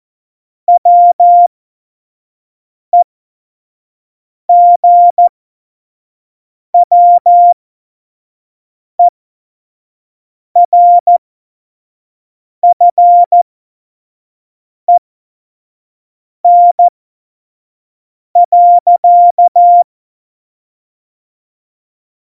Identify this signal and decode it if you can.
Morse code, heard as a keyed tone, about 14 wpm, spaced as for 5 wpm, 700 Hz. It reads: WEGWERFEN.
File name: Morsekurs / Witze / Witz 28 April 2020, 25 F70